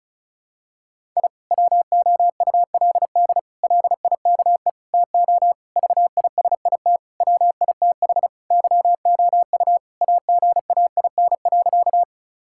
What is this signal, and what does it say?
Morse code, heard as a keyed tone, about 36 wpm, 700 Hz. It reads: I WOULD LIKE TO VISIT WITH YOU AGAIN.
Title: Sentences from Top 1000 Words - Review Part 6 - 35wpm